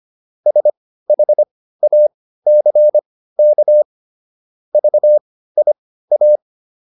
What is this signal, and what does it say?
Morse code, heard as a keyed tone, about 25 wpm, spaced as for 15 wpm, 600 Hz. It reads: SHACK VIA